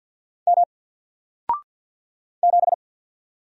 Morse code, heard as a keyed tone, about 50 wpm, 700 Hz. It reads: M 7
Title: Single Letter-Number 50wpm